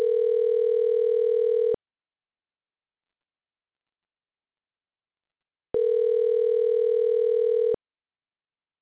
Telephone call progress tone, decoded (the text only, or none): ringback tone